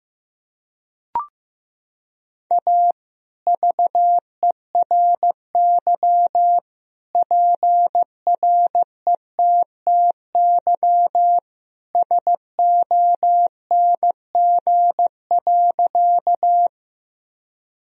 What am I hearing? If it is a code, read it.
Morse code, heard as a keyed tone, about 15 wpm, 700 Hz. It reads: A VERY PRETTY SONG.
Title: Sentences from Top 1000 Words - Review Part 2 - 15wpm